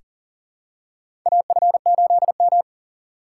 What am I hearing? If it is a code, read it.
Morse code, heard as a keyed tone, about 40 wpm, 700 Hz. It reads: AF8M
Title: USA Call Signs - 4 Characters - 4x Character Spacing - 40wpm